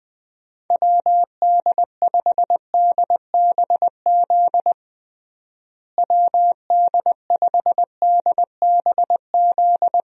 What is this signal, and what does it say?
Morse code, heard as a keyed tone, about 21 wpm, 700 Hz. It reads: WD5DBZ WD5DBZ
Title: USA Call Signs - 6 Characters - Repeated 2x - 20wpm